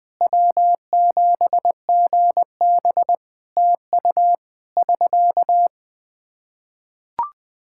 Morse code, heard as a keyed tone, about 20 wpm, 700 Hz. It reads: W7GB TU <SK>
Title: Practice QSO - Exchange 4 - Ending - 20wpm